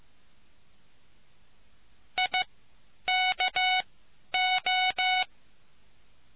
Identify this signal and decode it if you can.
Morse code, heard as a keyed tone, about 15 wpm, spaced as for 10 wpm, 740 Hz. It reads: IKO